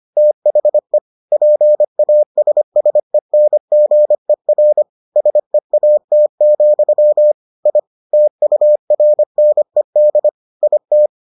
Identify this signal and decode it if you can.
Morse code, heard as a keyed tone, about 25 wpm, 600 Hz. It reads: THE PASSENGER SEAT, I TURNED IT